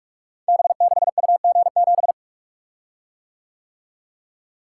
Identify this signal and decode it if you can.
Morse code, heard as a keyed tone, about 44 wpm, 700 Hz. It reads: B6UG7
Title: Code Groups - 5 Letter:Numbers - 45wpm